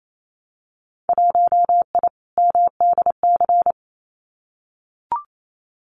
Morse code, heard as a keyed tone, about 28 wpm, 700 Hz. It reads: 1I MDC
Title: Winter Field Day Exchanges 28wpm